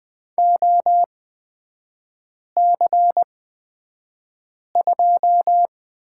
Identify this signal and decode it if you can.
Morse code, heard as a keyed tone, about 20 wpm, spaced as for 5 wpm, 700 Hz. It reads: OC2
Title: 243 Lesson 39e - Letter z - Three Characters 20wpm